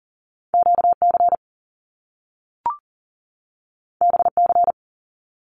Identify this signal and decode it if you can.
Morse code, heard as a keyed tone, about 40 wpm, 700 Hz. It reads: QC BC
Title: Canadian Provinces and Territories 40wpm